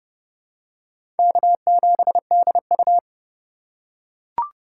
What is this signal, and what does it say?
Morse code, heard as a keyed tone, about 32 wpm, 700 Hz. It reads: K7DU
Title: USA Call Signs - 4 Characters - 3x Character Spacing - 30wpm